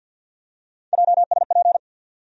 Morse code, heard as a keyed tone, about 47 wpm, 700 Hz. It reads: JSP
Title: USA Call Sign Postfixes - 3 Letters 50wpm